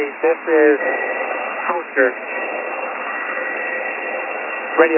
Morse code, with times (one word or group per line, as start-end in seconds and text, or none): none